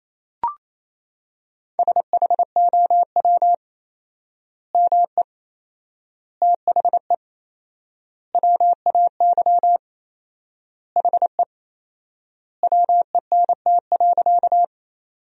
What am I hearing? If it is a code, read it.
Morse code, heard as a keyed tone, about 28 wpm, 700 Hz. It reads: SHOW ME THE WAY HE WENT.
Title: 4x Word-Spacing - Sentences from Top 200 Words - 28wpm